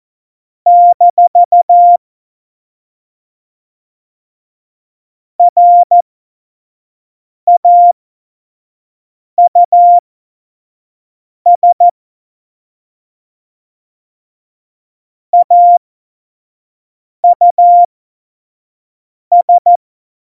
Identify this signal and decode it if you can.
Morse code, heard as a keyed tone, about 14 wpm, spaced as for 5 wpm, 700 Hz. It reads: - RAUS AUS
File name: Morsekurs / Witze / Witz 23. Oktober 2018, 25 F70